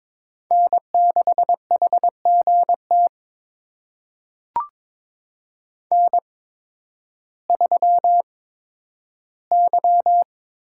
Morse code, heard as a keyed tone, about 22 wpm, spaced as for 12 wpm, 700 Hz. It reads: N6HGT N3Y
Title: USA Call Signs - 5 Characters - 8x Character Spacing - 22wpm